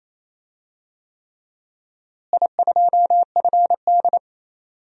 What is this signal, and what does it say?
Morse code, heard as a keyed tone, about 28 wpm, 700 Hz. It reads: I2FD
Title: DXCC-100 Call Signs - 4 Characters - Repeated 3x - 28wpm